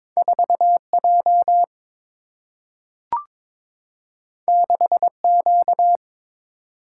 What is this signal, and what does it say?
Morse code, heard as a keyed tone, about 22 wpm, 700 Hz. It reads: 4J 6Q
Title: Single-Letter Single-Number 22wpm